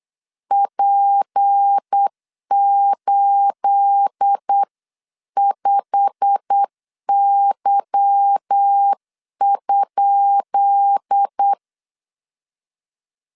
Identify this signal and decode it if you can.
Morse code, heard as a keyed tone, about 8 wpm, 800 Hz. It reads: P85Y?
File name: LEZIONI / 10 SIGNS / 10B INT 8wpm 3534